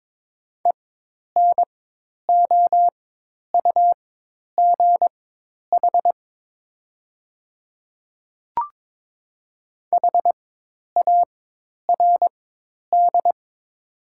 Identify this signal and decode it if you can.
Morse code, heard as a keyed tone, about 22 wpm, spaced as for 10 wpm, 700 Hz. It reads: ENOUGH HARD
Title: 4x Character Spacing - Top 500 Words - 22wpm